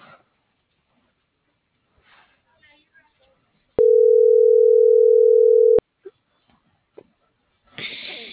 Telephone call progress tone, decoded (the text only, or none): ringback tone